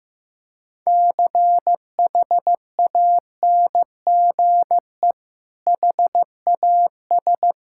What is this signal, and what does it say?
Morse code, heard as a keyed tone, about 15 wpm, 700 Hz. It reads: CHANGE HAS
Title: Sentences from Top 1600 Words 15wpm